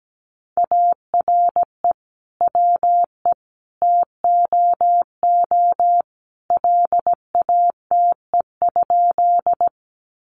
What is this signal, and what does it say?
Morse code, heard as a keyed tone, about 17 wpm, 700 Hz. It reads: ARE WE TOO LATE?
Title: Sentences from Top 300 Words 17wpm